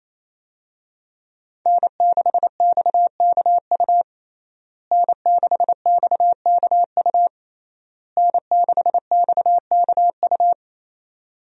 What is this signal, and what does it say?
Morse code, heard as a keyed tone, about 28 wpm, 700 Hz. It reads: N6XKU N6XKU N6XKU
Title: USA Call Signs - 5 Characters - Repeated 3x - 28wpm